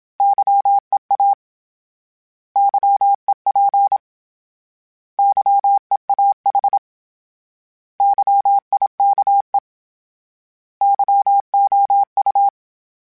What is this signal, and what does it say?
Morse code, heard as a keyed tone, about 27 wpm, 800 Hz. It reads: YEA YEP YEAH YIKE YOU
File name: Misc / Misc-mp3s / Misc @25